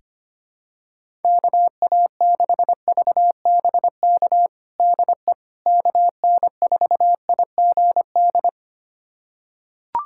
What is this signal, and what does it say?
Morse code, heard as a keyed tone, about 25 wpm, 700 Hz. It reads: KA6VBK DE KN4IGD
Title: USA Callsign DE Callsign 25wpm